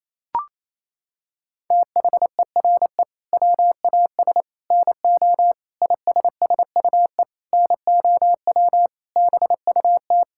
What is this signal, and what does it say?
Morse code, heard as a keyed tone, about 28 wpm, 700 Hz. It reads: THERE WAS NO ISSUE NOW BUT
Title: Sentences from Top 1000 Words - Review Part 6 - 28wpm